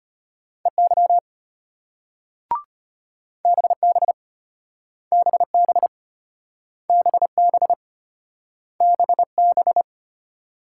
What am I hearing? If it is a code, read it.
Morse code, heard as a keyed tone, about 35 wpm, 700 Hz. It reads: EY BB BB BB BB